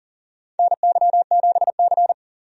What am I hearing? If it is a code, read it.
Morse code, heard as a keyed tone, about 40 wpm, 700 Hz. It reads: NY7C
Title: USA Call Signs - 4 Characters - 40wpm